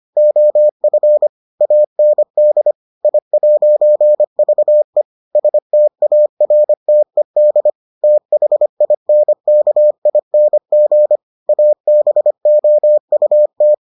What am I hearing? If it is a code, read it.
Morse code, heard as a keyed tone, about 25 wpm, 600 Hz. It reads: OF AND I'VE STARTED THINKING ABOUT